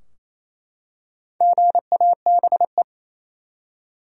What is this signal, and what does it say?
Morse code, heard as a keyed tone, about 28 wpm, 700 Hz. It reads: GABE
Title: USA Common Names - Max 4 Letters - 28wpm